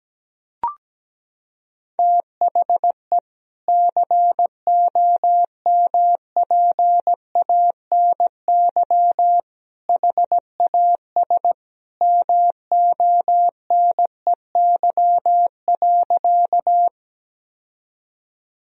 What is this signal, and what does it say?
Morse code, heard as a keyed tone, about 17 wpm, 700 Hz. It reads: THE COMPANY HAS MONEY.
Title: Sentences from Top 1000 Words - Review Part 3 - 17wpm